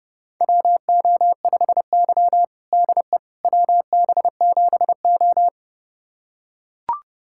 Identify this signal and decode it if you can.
Morse code, heard as a keyed tone, about 30 wpm, 700 Hz. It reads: WO5Y DE WB7O
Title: USA Callsign DE Callsign - 4 Characters - 30wpm